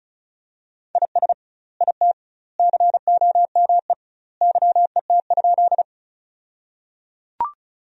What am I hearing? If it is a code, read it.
Morse code, heard as a keyed tone, about 36 wpm, 700 Hz. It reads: IS IT COME YET?
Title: 2x Word-Spacing - Sentences from Top 500 Words - 35wpm